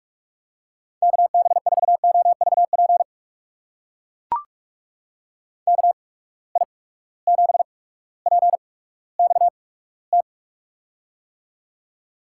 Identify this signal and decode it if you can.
Morse code, heard as a keyed tone, about 43 wpm, spaced as for 19 wpm, 700 Hz. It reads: KB4OVP KI7PXT
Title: USA Call Signs - 6 Characters - 8x Character Spacing - 45wpm